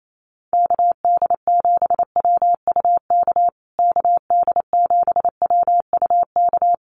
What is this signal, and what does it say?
Morse code, heard as a keyed tone, about 28 wpm, 700 Hz. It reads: KD7WUK KD7WUK
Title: USA Call Signs Repeated 3x 28wpm